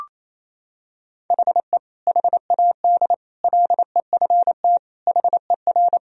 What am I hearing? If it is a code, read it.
Morse code, heard as a keyed tone, about 28 wpm, 700 Hz. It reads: HE HAD LEFT HER
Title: Sentences from Top 1000 Words - Review Part 3 - 28wpm